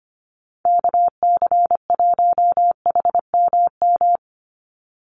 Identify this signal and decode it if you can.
Morse code, heard as a keyed tone, about 25 wpm, 700 Hz. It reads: KC1HMM
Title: USA Call Signs 25wpm